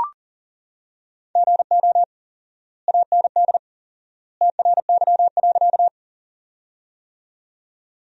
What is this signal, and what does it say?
Morse code, heard as a keyed tone, about 40 wpm, 700 Hz. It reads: GO AND TRY.
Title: 4x Word-Spacing - Sentences from Top 200 Words - 40wpm